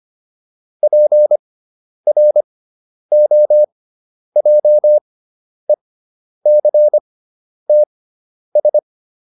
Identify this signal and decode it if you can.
Morse code, heard as a keyed tone, about 25 wpm, spaced as for 10 wpm, 600 Hz. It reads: PROJECTS